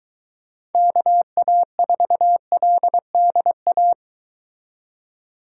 Morse code, heard as a keyed tone, about 23 wpm, 700 Hz. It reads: KA4LDA